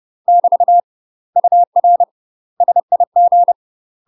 Morse code, heard as a keyed tone, about 30 wpm, 700 Hz. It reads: <BT> UR SIG
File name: UK MASTER QSOs TRAINING / 30WPM / UK MASTER QSOs 0000